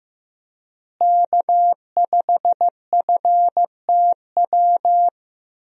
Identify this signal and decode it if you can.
Morse code, heard as a keyed tone, about 15 wpm, 700 Hz. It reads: K5FTW
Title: USA Call Signs - 5 Characters - Repeated 3x - 15wpm